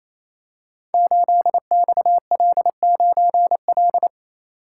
Morse code, heard as a keyed tone, about 28 wpm, 700 Hz. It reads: 8XL9L
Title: Code Groups - 5 Letter:Numbers - 28wpm